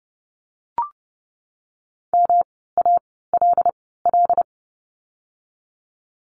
Morse code, heard as a keyed tone, about 30 wpm, spaced as for 17 wpm, 700 Hz. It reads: MALL
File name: course / 30:10wpm / 108 Lesson 18f - Letter m - Words 30wpm